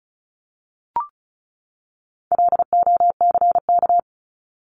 Morse code, heard as a keyed tone, about 34 wpm, 700 Hz. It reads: LOCK
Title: Top 1001-1250 Words 35wpm